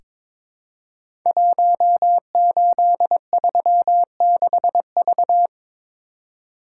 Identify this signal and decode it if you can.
Morse code, heard as a keyed tone, about 22 wpm, 700 Hz. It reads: 1836V
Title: Code Groups - 5 Letter:Numbers - 22wpm